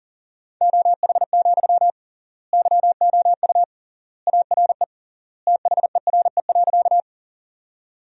Morse code, heard as a keyed tone, about 40 wpm, 700 Hz. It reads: OH, YOU ARE THERE.
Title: 3x Word-Spacing - Sentences from Top 500 Words - 40wpm